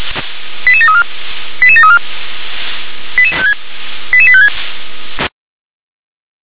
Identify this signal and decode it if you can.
Five-tone selective call (ZVEI), heard as quick runs of five tones, tones of about 80 ms. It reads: 88734 88734 88856